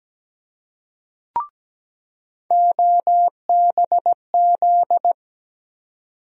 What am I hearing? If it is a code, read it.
Morse code, heard as a keyed tone, about 17 wpm, 700 Hz. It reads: OBZ
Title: USA Call Sign Postfixes - 3 Letters 17wpm